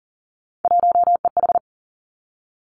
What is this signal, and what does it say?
Morse code, heard as a keyed tone, about 42 wpm, 700 Hz. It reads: 1EH
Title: Three Challenging Letter-Number Combinations 40wpm